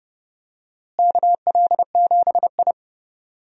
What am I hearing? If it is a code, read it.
Morse code, heard as a keyed tone, about 30 wpm, 700 Hz. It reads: KL7I
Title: DXCC-100 Call Signs - 4 Characters - Repeated 3x - 30wpm